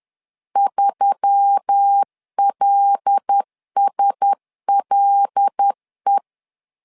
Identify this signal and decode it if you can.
Morse code, heard as a keyed tone, about 10 wpm, 800 Hz. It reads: 3LSLE
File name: LEZIONI / 5 LBVC / 5 L 10wpm 2259